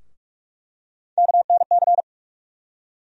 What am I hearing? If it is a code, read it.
Morse code, heard as a keyed tone, about 46 wpm, 700 Hz. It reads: KNC
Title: USA Call Sign Postfixes - 3 Letters 45wpm